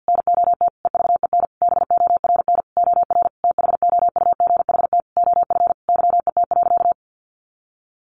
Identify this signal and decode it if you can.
Morse code, heard as a keyed tone, about 50 wpm, 700 Hz. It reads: NOT EVEN BORN OR THOUGHT OF YET.
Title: A Sound Of Thunder 50wpm